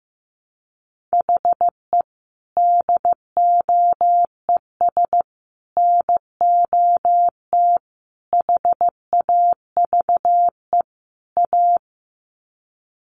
Morse code, heard as a keyed tone, about 15 wpm, 700 Hz. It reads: HE DOES NOT HAVE A
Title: Sets of 5 Words Encore 15wpm